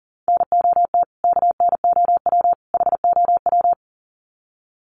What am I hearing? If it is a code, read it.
Morse code, heard as a keyed tone, about 40 wpm, 700 Hz. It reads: NOT KNOW HOW
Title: Sets of 3 Words 40wpm